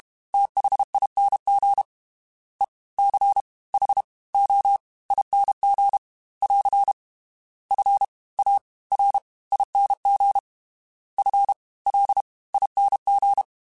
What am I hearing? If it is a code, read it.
Morse code, heard as a keyed tone, about 32 wpm, 800 Hz. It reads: THING E C H O ING <AR> F A R ING F L ING
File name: Special INGs / Special INGs-mp3s / Fast ing @30